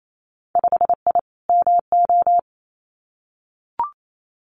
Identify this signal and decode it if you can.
Morse code, heard as a keyed tone, about 28 wpm, 700 Hz. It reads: 5I MO
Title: Winter Field Day Exchanges Encore 28wpm